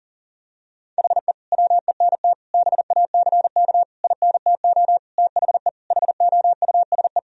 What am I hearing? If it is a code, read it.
Morse code, heard as a keyed tone, about 40 wpm, 700 Hz. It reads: HE WENT BACK INTO THE HOUSE
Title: Sentences from Top 1000 Words - Review Part 2 - 40wpm